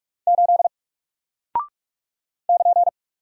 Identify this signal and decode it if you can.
Morse code, heard as a keyed tone, about 48 wpm, 700 Hz. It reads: 8 <KN>
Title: Single Character (Letter:Number:Punctuation:Prosign) Alt 45wpm